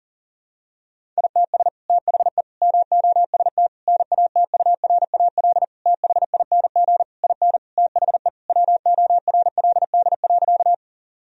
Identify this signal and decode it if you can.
Morse code, heard as a keyed tone, about 40 wpm, 700 Hz. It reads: ITS THE MOST NATURAL THING IN THE WORLD.
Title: Sentences from Top 1000 Words - Review Part 4 - 40wpm